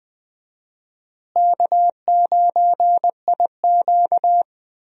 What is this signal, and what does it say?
Morse code, heard as a keyed tone, about 20 wpm, 700 Hz. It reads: K9IQ
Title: USA Call Signs - 4 Characters - Repeated 2x - 20wpm